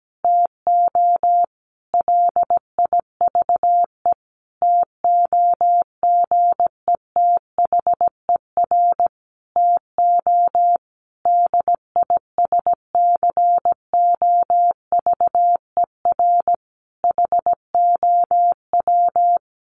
Morse code, heard as a keyed tone, about 17 wpm, 700 Hz. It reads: TO LIVE TOGETHER TO DISCOVER HOW
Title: Sentences from Top 1100 Words 17wpm